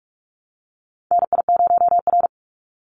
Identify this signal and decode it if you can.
Morse code, heard as a keyed tone, about 46 wpm, 700 Hz. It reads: NI0R
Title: Call Signs - 45wpm